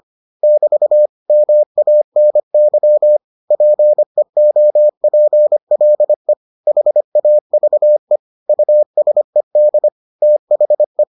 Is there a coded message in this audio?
Yes: Morse code from 0.4 to 11.0 s, heard as a keyed tone, about 25 wpm, 600 Hz. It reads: <BT> MANY PEOPLE HAVE USED THE